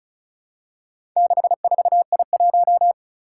Morse code, heard as a keyed tone, about 36 wpm, 700 Hz. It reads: 64I1
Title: Four Challenging Letter-Number Combinations 35wpm